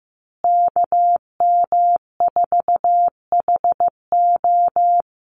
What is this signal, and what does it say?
Morse code, heard as a keyed tone, about 15 wpm, 700 Hz. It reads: KM4HO
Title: USA Call Signs Repeated 3x 15wpm